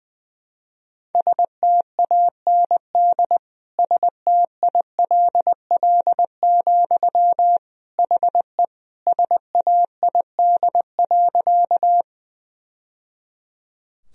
Morse code, heard as a keyed tone, about 20 wpm, 700 Hz. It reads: STAND STILL, HE SAID.
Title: Sentences from Top 1000 Words - Review Part 1 - 20wpm